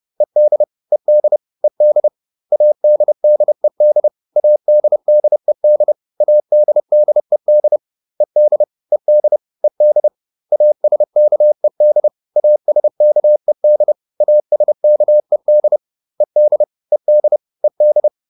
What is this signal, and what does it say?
Morse code, heard as a keyed tone, about 30 wpm, 600 Hz. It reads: ED ED ED ADDED ADDED ADDED ED ED ED ASKED ASKED ASKED ED ED ED